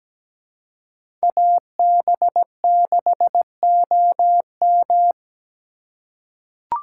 Morse code, heard as a keyed tone, about 17 wpm, 700 Hz. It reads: AB6OM
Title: USA Call Signs - 5 Characters - 3x Character Spacing - 17wpm